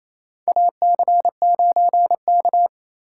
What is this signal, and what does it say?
Morse code, heard as a keyed tone, about 27 wpm, 700 Hz. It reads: AC9K
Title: USA Call Signs - 4 Characters - 4x Character Spacing - 28wpm